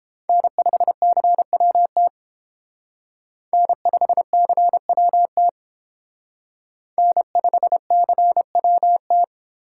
Morse code, heard as a keyed tone, about 29 wpm, 700 Hz. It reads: N5CWT N5CWT N5CWT